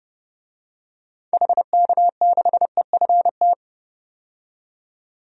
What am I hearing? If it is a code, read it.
Morse code, heard as a keyed tone, about 32 wpm, 700 Hz. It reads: HK6EFT
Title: DXCC-100 Call Signs - 6 Characters - 30wpm